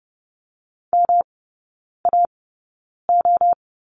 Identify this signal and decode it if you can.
Morse code, heard as a keyed tone, about 30 wpm, spaced as for 9 wpm, 700 Hz. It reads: MAO
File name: Lesson 18e - Letter m - Three Characters 30wpm